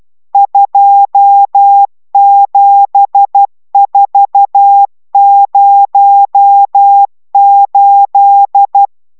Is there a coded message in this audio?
Yes: Morse code from 0.3 to 8.9 s, heard as a keyed tone, about 12 wpm, 800 Hz. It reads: 27408